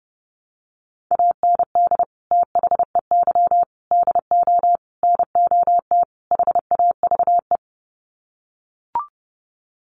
Morse code, heard as a keyed tone, about 28 wpm, 700 Hz. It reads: AND THEY DO NOT HAVE
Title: Sets of 5 Words Encore2 30wpm